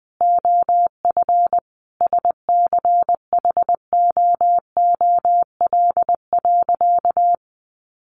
Morse code, heard as a keyed tone, about 20 wpm, 700 Hz. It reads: OF SCHOOL.
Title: Additional Easy Phrases 20wpm